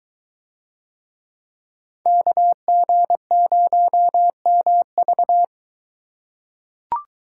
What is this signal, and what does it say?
Morse code, heard as a keyed tone, about 21 wpm, 700 Hz. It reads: KG0MV